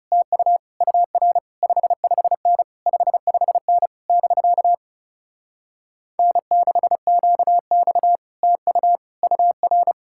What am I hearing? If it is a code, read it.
Morse code, heard as a keyed tone, about 38 wpm, 700 Hz. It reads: TU UR 55N 55N <BK> N6QX TU UR
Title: Speed-Racing Common QSO Phrases - 30wpm